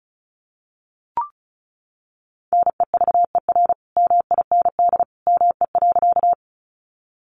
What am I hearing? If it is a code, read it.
Morse code, heard as a keyed tone, about 34 wpm, 700 Hz. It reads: NEVER MIND ME.
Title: Sentences from Top 500 Words 35wpm